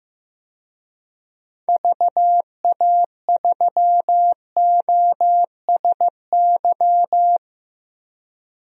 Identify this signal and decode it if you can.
Morse code, heard as a keyed tone, about 15 wpm, 700 Hz. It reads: VA3OSY
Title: DXCC-100 Call Signs - 6 Characters - Repeated 2x - 15wpm